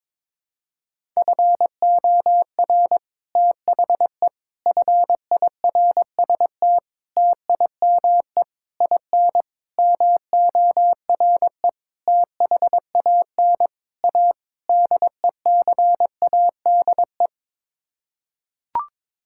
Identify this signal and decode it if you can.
Morse code, heard as a keyed tone, about 21 wpm, 700 Hz. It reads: FOR THE FIRST TIME IN MORE THAN A DECADE
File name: Sets of 9 Words - 22wpm